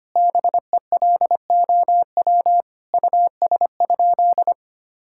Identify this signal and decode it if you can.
Morse code, heard as a keyed tone, about 25 wpm, 700 Hz. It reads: BELOW US?
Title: Sentences from Top 1000 Words - Review Part 5 - 25wpm